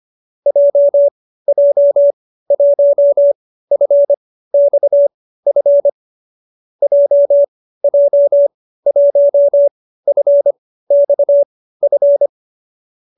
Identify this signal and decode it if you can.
Morse code, heard as a keyed tone, about 25 wpm, spaced as for 15 wpm, 570 Hz. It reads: JJ1FXF JJ1FXF